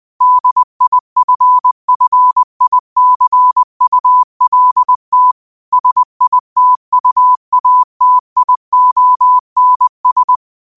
Morse code, heard as a keyed tone, about 20 wpm, 1000 Hz. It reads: DIFFICULT SITUATIONS